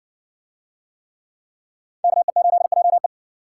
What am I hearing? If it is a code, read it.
Morse code, heard as a keyed tone, about 58 wpm, 700 Hz. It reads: KE8JE